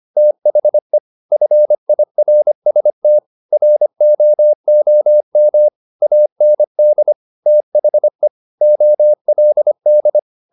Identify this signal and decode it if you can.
Morse code, heard as a keyed tone, about 25 wpm, 600 Hz. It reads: THE FIRST ROOM AND THE OLD
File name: the old man and the sea part 1 25wpm